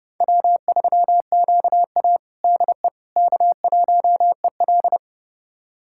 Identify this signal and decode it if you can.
Morse code, heard as a keyed tone, about 30 wpm, 700 Hz. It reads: W3QA DE K1EL
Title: USA Callsign DE Callsign - 4 Characters - 30wpm